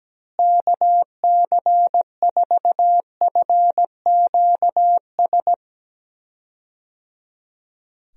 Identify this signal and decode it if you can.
Morse code, heard as a keyed tone, about 17 wpm, 700 Hz. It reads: KC4FQS
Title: USA Call Signs - 6 Characters - Repeated 3x - 17wpm